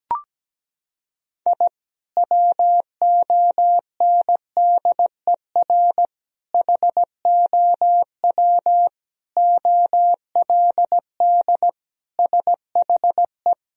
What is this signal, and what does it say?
Morse code, heard as a keyed tone, about 17 wpm, 700 Hz. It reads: I WONDER HOW OLD SHE
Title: Sentences from Top 1000 Words - Review Part 3 - 17wpm